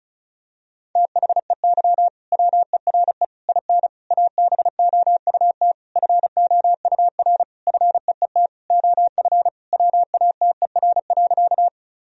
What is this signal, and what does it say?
Morse code, heard as a keyed tone, about 34 wpm, 700 Hz. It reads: THEY WERE IN ABOUT FOUR FEET OF WATER.
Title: Sentences from Top 1000 Words - Review Part 1 - 35wpm